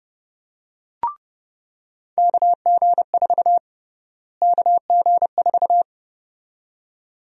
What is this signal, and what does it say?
Morse code, heard as a keyed tone, about 30 wpm, 700 Hz. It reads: KG4 KG4
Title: USA Call Sign Prefixes - Repeated 2x - 30wpm